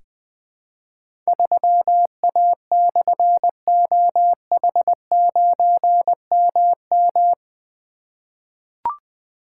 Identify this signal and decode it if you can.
Morse code, heard as a keyed tone, about 21 wpm, 700 Hz. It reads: 3A/OH9MM
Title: Difficult Call Signs 20wpm